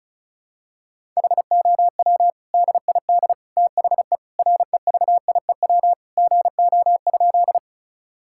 Morse code, heard as a keyed tone, about 34 wpm, 700 Hz. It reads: HOW DID THE REVIEW GO?
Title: Sentences from Top 1000 Words - Review Part 6 - 35wpm